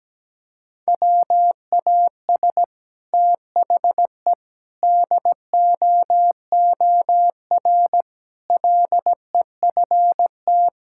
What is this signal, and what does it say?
Morse code, heard as a keyed tone, about 17 wpm, 700 Hz. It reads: WAS THE DOOR LEFT
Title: Sentences from Top 1000 Words - Review Part 6 - 17wpm